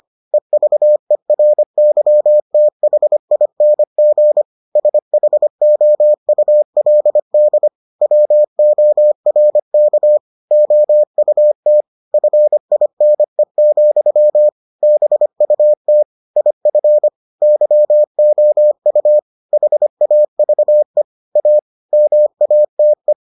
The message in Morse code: EVERYTHING SHOULD WORK OUT FINE, BUT IF YOU HAVE A MATE

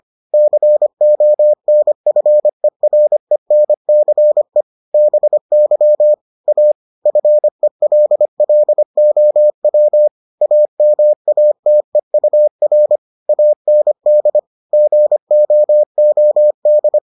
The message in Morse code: CONFERENCE BY A FELLOW AMATEUR AND GOOD

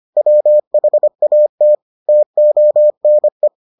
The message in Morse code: WHAT TONE

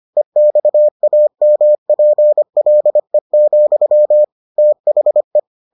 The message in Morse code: EXAMPLE, THE

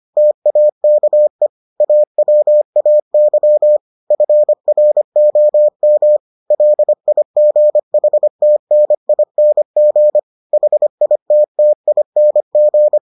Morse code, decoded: TAKE AWAY FROM LIGHTNING HITTING

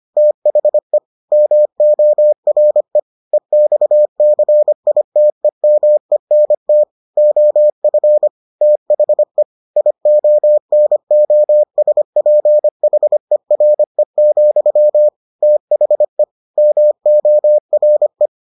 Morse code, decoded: THE MORE EXCITEMENT OF THE IONOSPHERE, THE MORE

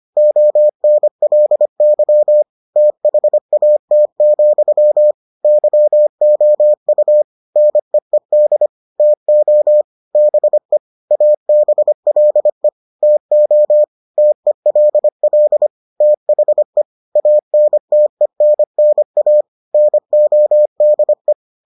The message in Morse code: ONLY THAT, YOU NEED TO BE ABLE TO TELL THE ANTENNA NODE

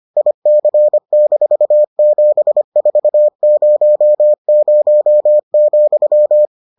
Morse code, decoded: IC-7400,

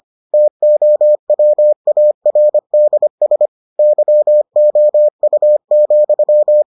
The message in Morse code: TOWARDS YOU,

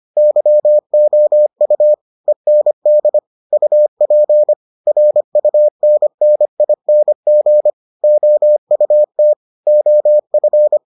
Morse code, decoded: YOU END UP RUNNING OUT OF

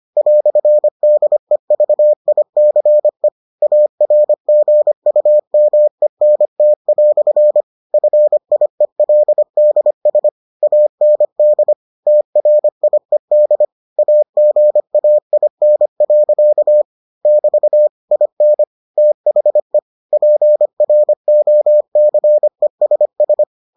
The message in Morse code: "DEVICE ARGUMENT" FIELDS AND TRIED AGAIN. <BT> IN THE PROCESS

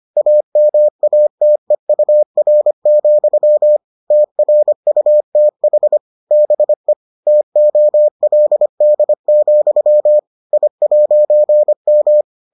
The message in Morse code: AMATEUR, TRUTH BE TOLD, I'M

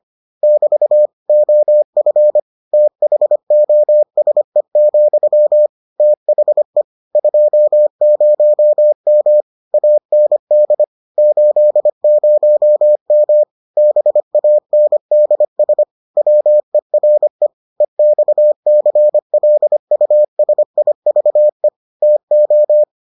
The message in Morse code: <BT> OF THOSE, THE 20M AND 80M BANDS WERE EXCLUSIVE TO

